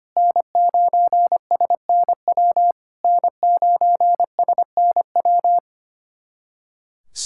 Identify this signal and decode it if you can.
Morse code, heard as a keyed tone, about 25 wpm, 700 Hz. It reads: N9SNW N9SNW
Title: SOTA QSO Practice 25wpm